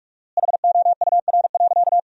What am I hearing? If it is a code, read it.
Morse code, heard as a keyed tone, about 46 wpm, 700 Hz. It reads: HOUR.
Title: 4x Word-Spacing - Sentences from Top 500 Words - 45wpm